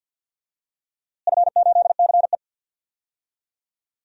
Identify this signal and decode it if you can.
Morse code, heard as a keyed tone, about 53 wpm, 700 Hz. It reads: F8CE
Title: DXCC-100 Call Signs - 4 Characters - Repeated 3x - 50wpm